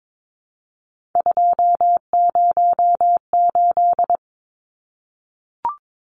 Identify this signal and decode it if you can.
Morse code, heard as a keyed tone, about 22 wpm, 700 Hz. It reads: 208
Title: Numbers - 3-Digit 22wpm